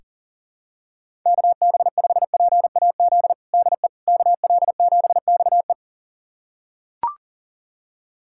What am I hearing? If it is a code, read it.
Morse code, heard as a keyed tone, about 37 wpm, 700 Hz. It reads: KB5PAZ DE KL7XE
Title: USA Callsign DE Callsign 40wpm